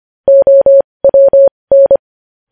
Morse code, heard as a keyed tone, about 25 wpm, spaced as for 20 wpm, 550 Hz. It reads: OWN
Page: news-headlines-in-morse-code-at-20-wpm